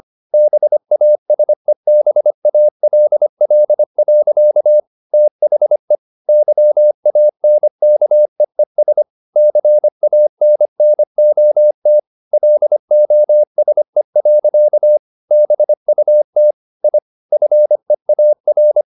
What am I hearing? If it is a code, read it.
Morse code, heard as a keyed tone, about 25 wpm, 600 Hz. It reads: BASEBALL. THE YANKEES CANNOT LOSE. BUT I FEAR
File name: the old man and the sea part 1 25wpm